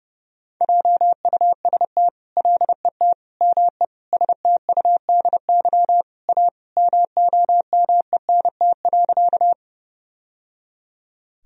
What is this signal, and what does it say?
Morse code, heard as a keyed tone, about 30 wpm, 700 Hz. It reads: JUST LET ME STUDY A MOMENT.
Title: Sentences from Top 1000 Words - Review Part 1 - 30wpm